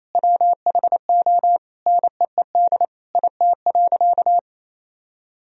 Morse code, heard as a keyed tone, about 28 wpm, 700 Hz. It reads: WHO NEED IT.
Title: Sentences from Top 1000 Words - Review Part 4 - 28wpm